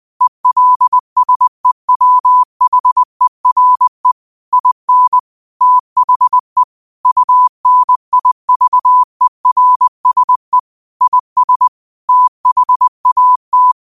Morse code, heard as a keyed tone, about 20 wpm, 1000 Hz. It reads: ELSEWHERE IN THE UNIVERSE IS THAT